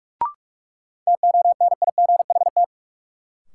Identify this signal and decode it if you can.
Morse code, heard as a keyed tone, about 42 wpm, 700 Hz. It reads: TONIGHT
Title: Sentences from Top 600 Words - New Word Drill - 45wpm